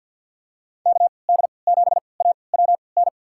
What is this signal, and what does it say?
Morse code, heard as a keyed tone, about 52 wpm, spaced as for 29 wpm, 700 Hz. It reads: KD7AWN
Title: USA Call Signs - 6 Characters - 3x Character Spacing - 50wpm